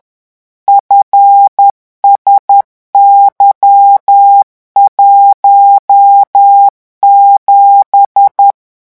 Morse code, heard as a keyed tone, about 10 wpm, 800 Hz. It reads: FSY17
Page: REVISION 18 - 10 mots minute